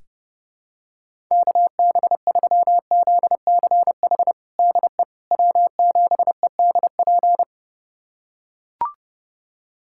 Morse code, heard as a keyed tone, about 30 wpm, 700 Hz. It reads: KB3ZCH DE W7EDP
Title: USA Callsign DE Callsign 30wpm